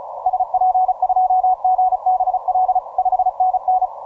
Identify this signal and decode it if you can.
Morse code, heard as a keyed tone, about 34 wpm, 740 Hz. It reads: SP2GBL 5NN
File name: simple-file-list / VP6R 17MCW